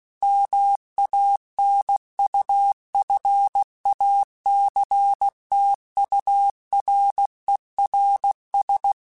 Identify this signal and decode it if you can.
Morse code, heard as a keyed tone, about 16 wpm, 800 Hz. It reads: MANUFACTURERS
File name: Quotes / Quotes-mp3s / Airtight @15